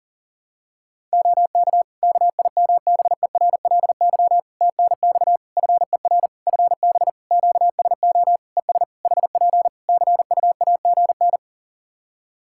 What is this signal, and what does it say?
Morse code, heard as a keyed tone, about 40 wpm, 700 Hz. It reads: OK KIMBERLY TNX FER FB QSO ES HP CUAGN
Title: Practice QSO - Exchange 4 - Ending - 40wpm